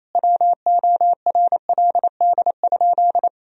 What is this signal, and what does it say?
Morse code, heard as a keyed tone, about 28 wpm, 700 Hz. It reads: WORLD?
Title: Sentences from Top 1000 Words - Review Part 1 - 28wpm